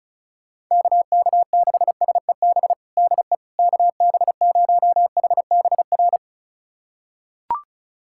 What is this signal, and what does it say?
Morse code, heard as a keyed tone, about 36 wpm, 690 Hz. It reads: KK6SEB DE KB0HBR
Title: USA Callsign DE Callsign - 6 Characters - 35wpm